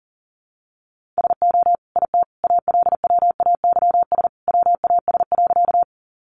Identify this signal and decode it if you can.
Morse code, heard as a keyed tone, about 40 wpm, 700 Hz. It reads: SO IT ALWAYS WAS.
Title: Sentences from Top 300 Words 40wpm